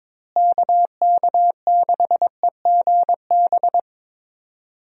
Morse code, heard as a keyed tone, about 22 wpm, 700 Hz. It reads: KK6EGB
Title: USA Call Signs - 6 Characters - Repeated 3x - 22wpm